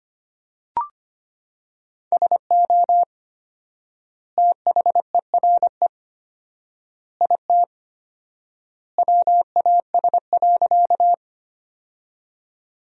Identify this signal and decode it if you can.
Morse code, heard as a keyed tone, about 25 wpm, 700 Hz. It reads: SO THERE IT WAS.
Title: 4x Word-Spacing - Sentences from Top 100 Words - 25wpm